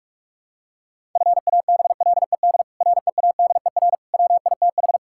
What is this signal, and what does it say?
Morse code, heard as a keyed tone, about 46 wpm, 700 Hz. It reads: FABLED READER WITH
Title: Pangram Challenge v2 - 45wpm